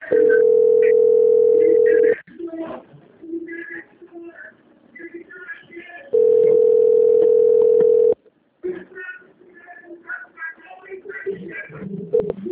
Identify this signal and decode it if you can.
Ringback tone (North America), 440 + 480 Hz.